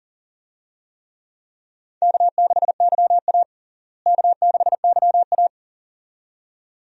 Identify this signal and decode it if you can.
Morse code, heard as a keyed tone, about 40 wpm, 700 Hz. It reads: K6YA K6YA
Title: USA Call Signs - 4 Characters - Repeated 2x - 40wpm